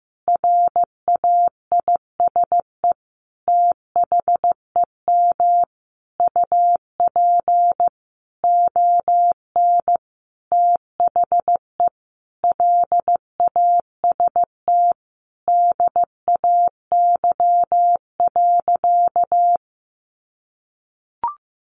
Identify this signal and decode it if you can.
Morse code, heard as a keyed tone, about 14 wpm, 700 Hz. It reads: RAISE THEM UP ON THE LAST DAY.
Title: Sentences from Top 1400 Words 15wpm